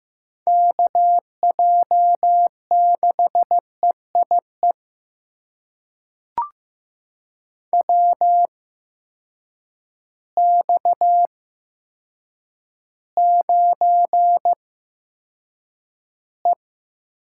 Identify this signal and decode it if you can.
Morse code, heard as a keyed tone, about 15 wpm, spaced as for 8 wpm, 700 Hz. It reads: KJ6EIE WX9E
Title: USA Call Signs - 8x Character Spacing - 15wpm